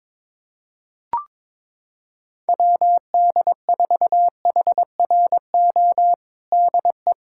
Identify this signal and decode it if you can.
Morse code, heard as a keyed tone, about 22 wpm, 700 Hz. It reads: WD4HRO DE
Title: USA Callsign DE Callsign 22wpm